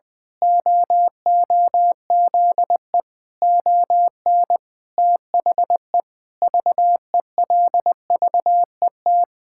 Morse code, heard as a keyed tone, about 19 wpm, 700 Hz. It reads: OOZE ON THE VELVET
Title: Pangram Challenge v2 - 20wpm